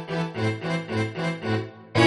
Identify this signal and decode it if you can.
There is no code.